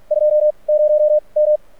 Morse code, heard as a keyed tone, about 48 wpm, 600 Hz. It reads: 2 0 M
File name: archived / CRSnet